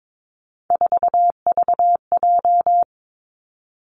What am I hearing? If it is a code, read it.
Morse code, heard as a keyed tone, about 22 wpm, 700 Hz. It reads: 4VJ
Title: Three Challenging Letter-Number Combinations 22wpm